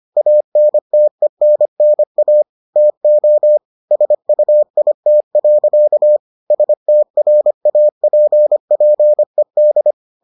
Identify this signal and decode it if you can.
Morse code, heard as a keyed tone, about 25 wpm, 600 Hz. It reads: ANTENNA TO SUIT. STRAPPED